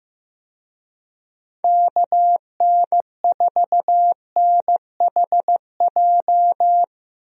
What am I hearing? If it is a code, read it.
Morse code, heard as a keyed tone, about 15 wpm, 700 Hz. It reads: KN4NHJ
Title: USA Call Signs - 6 Characters - Repeated 2x - 15wpm